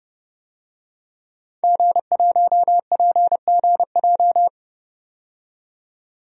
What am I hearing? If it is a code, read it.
Morse code, heard as a keyed tone, about 32 wpm, 700 Hz. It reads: G1PGJ
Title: DXCC-100 Call Signs - 5 Characters - 30wpm